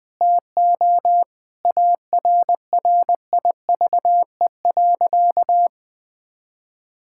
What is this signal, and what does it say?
Morse code, heard as a keyed tone, about 20 wpm, 700 Hz. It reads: TO ARRIVE.
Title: Sentences with 6-Word Phrases - Spoken Beginning - 20wpm